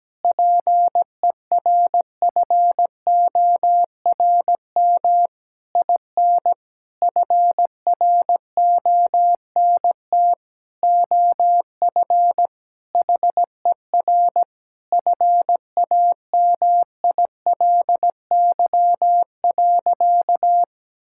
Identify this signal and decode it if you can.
Morse code, heard as a keyed tone, about 17 wpm, 700 Hz. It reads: PERFORM IN FRONT OF HER FAMILY.
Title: Sentences from Top 1000 Words - Review Part 5 - 17wpm